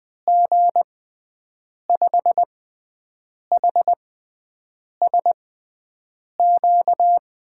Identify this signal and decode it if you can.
Morse code, heard as a keyed tone, about 20 wpm, spaced as for 7 wpm, 700 Hz. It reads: G5HSQ
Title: USA Call Signs - 6 Characters - 6x Character Spacing - 20wpm